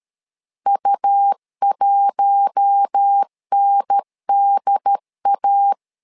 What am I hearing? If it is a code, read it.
Morse code, heard as a keyed tone, about 13 wpm, 800 Hz. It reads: U1NDA
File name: LEZIONI / 5 LBVC / 5 L 12wpm 1913